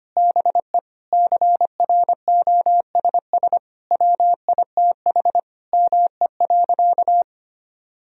Morse code, heard as a keyed tone, about 25 wpm, 700 Hz. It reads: BE CROSS WITH ME.
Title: Sentences from Top 1000 Words - Review Part 1 - 25wpm